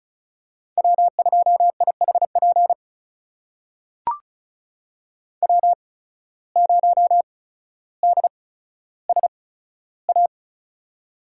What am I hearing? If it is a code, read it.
Morse code, heard as a keyed tone, about 36 wpm, spaced as for 15 wpm, 700 Hz. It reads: W2IHP W0DSA
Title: USA Call Signs - 5 Characters - 8x Character Spacing - 35wpm